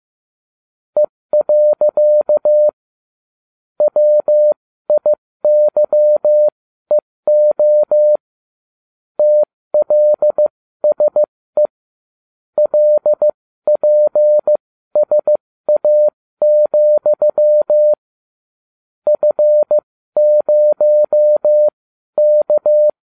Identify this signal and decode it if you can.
Morse code, heard as a keyed tone, about 15 wpm, spaced as for 13 wpm, 600 Hz. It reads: E. WIYEO TLSE LPSA, F0K